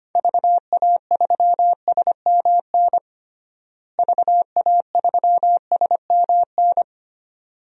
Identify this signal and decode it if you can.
Morse code, heard as a keyed tone, about 25 wpm, 700 Hz. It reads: VA3SMN VA3SMN
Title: DXCC-100 Call Signs - 6 Characters - Repeated 2x - 25wpm